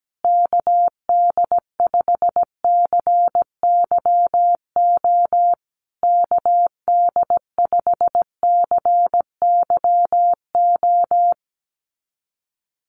Morse code, heard as a keyed tone, about 17 wpm, 700 Hz. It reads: KD5CYO KD5CYO
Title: USA Call Signs Repeated 3x 17wpm